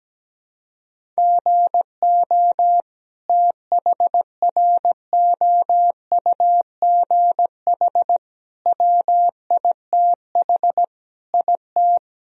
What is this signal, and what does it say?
Morse code, heard as a keyed tone, about 17 wpm, 700 Hz. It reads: GO THROUGH WITH IT
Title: Sentences from Top 1000 Words - Review Part 1 - 17wpm